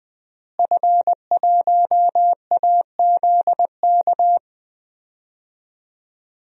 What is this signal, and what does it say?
Morse code, heard as a keyed tone, about 20 wpm, 700 Hz. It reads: F1AZK
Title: DXCC-100 Call Signs - 5 Characters - Repeated 2x - 20wpm